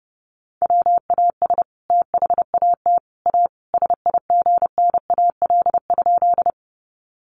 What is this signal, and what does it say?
Morse code, heard as a keyed tone, about 30 wpm, 700 Hz. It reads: WAS THAT A SIGNAL?
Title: Sentences from Top 1400 Words 30wpm